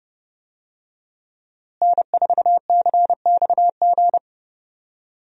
Morse code, heard as a keyed tone, about 30 wpm, 700 Hz. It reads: N4CXG
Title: USA Call Signs - 5 Characters - Repeated 3x - 30wpm